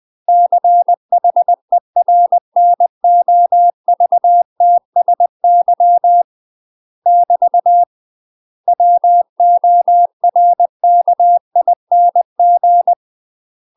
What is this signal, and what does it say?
Morse code, heard as a keyed tone, about 20 wpm, 700 Hz. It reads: CHERNOVTSY <BT> WORKING
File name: UK MASTER QSOs TRAINING / 20WPM / UK MASTER QSOs 0008